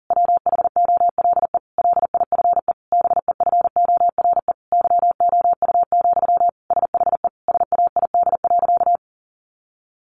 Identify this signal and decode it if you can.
Morse code, heard as a keyed tone, about 40 wpm, 700 Hz. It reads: WHOLE LIFE BEFORE YOU, SHE SAID.
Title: Sentences from Top 400 Words 40wpm